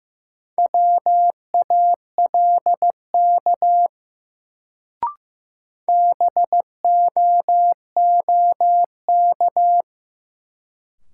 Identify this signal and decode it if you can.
Morse code, heard as a keyed tone, about 15 wpm, 700 Hz. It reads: WALK BOOK
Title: Sentences from Top 300 Words - New Word Drill - 15wpm